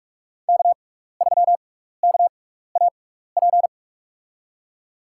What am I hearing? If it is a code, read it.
Morse code, heard as a keyed tone, about 46 wpm, spaced as for 15 wpm, 700 Hz. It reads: K3KAP
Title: USA Call Signs - 6x Character Spacing - 45wpm